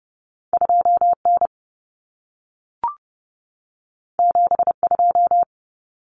Morse code, heard as a keyed tone, about 30 wpm, 700 Hz. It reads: 2N 72
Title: Common Cut Numbers - 2-Digit 30wpm